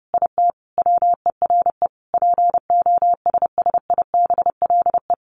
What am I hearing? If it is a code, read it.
Morse code, heard as a keyed tone, about 30 wpm, 700 Hz. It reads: IT WERE POSSIBLE
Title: Sentences from Top 500 Words 30wpm